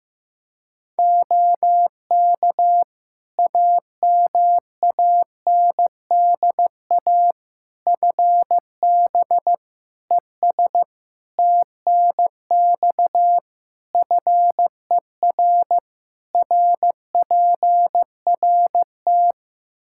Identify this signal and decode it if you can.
Morse code, heard as a keyed tone, about 16 wpm, 700 Hz. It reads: OK AMANDA FB ES TNX FER RPRT
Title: K1USN SST QSO Practice 15wpm